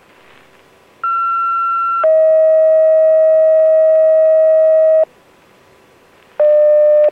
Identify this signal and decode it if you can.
Two-tone page, heard as a one-second tone, then a three-second tone, about 1360 then 640 Hz. It is heard from 1.0 to 5.0 s.